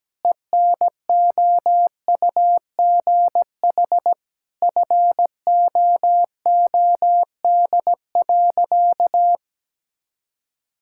Morse code, heard as a keyed tone, about 17 wpm, 700 Hz. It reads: ENOUGH FOOD.
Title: Sentences from Top 1000 Words - Review Part 5 - 17wpm